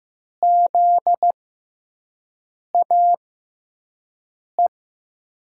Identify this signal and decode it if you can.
Morse code, heard as a keyed tone, about 15 wpm, spaced as for 5 wpm, 700 Hz. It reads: ZAE
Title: USA Call Signs - 6 Characters - 6x Character Spacing - 15wpm